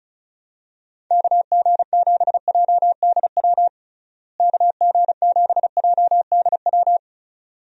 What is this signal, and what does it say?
Morse code, heard as a keyed tone, about 34 wpm, 700 Hz. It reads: KG7JDW KG7JDW
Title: USA Call Signs - 6 Characters - Repeated 2x - 35wpm